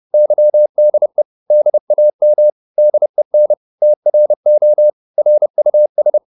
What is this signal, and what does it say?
Morse code, heard as a keyed tone, about 30 wpm, 600 Hz. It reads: YDE DAM DEN TRO RUS
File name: Korte ord DK 30wpm